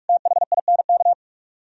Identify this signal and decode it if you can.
Morse code, heard as a keyed tone, about 57 wpm, 700 Hz. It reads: THINK